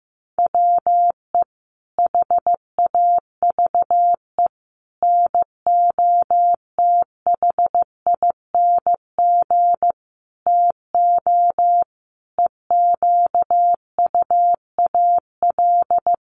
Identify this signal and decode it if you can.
Morse code, heard as a keyed tone, about 14 wpm, 700 Hz. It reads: WE HAVE NOTHING TO EQUAL
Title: Sentences from Top 1400 Words 15wpm